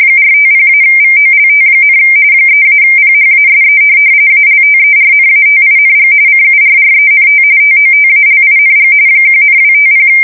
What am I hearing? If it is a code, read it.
RTTY, about 45.45 baud. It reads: 450-VOLT, 3-PHASE ALTERNATING CURRENT TOTALING 1,250
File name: fsk170 / USS-New-Jersey